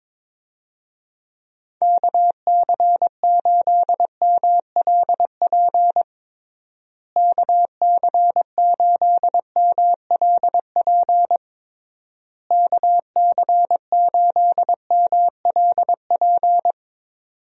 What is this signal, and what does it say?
Morse code, heard as a keyed tone, about 22 wpm, 700 Hz. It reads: KC8MLP KC8MLP KC8MLP
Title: USA Call Signs - 6 Characters - Repeated 3x - 22wpm